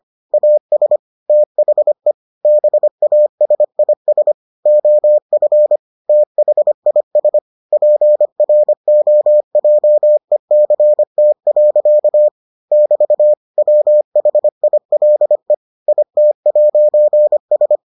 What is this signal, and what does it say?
Morse code, heard as a keyed tone, about 25 wpm, 600 Hz. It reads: AS THE BASIS OF THIS PROJECT. <BT> WHILE IT'S